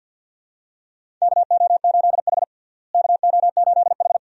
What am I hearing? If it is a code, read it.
Morse code, heard as a keyed tone, about 47 wpm, 700 Hz. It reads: KO8H KO8H
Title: USA Call Signs - 4 Characters - Repeated 2x - 50wpm